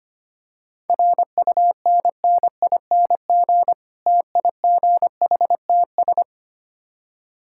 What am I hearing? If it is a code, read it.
Morse code, heard as a keyed tone, about 25 wpm, 700 Hz. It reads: RUNNING TIGHTS
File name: practice / 25wpm / Difficult Words with Hints - 25wpm